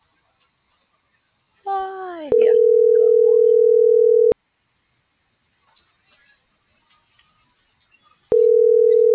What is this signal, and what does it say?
Ringback tone (North America), 440 + 480 Hz.